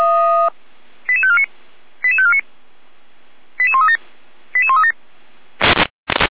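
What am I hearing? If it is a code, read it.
Five-tone selective call (ZVEI), heard as quick runs of five tones, tones of about 70 ms. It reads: 88549 88549 88137 88137
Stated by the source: ZVEI five-tone selective call to address 88415, received via SDR.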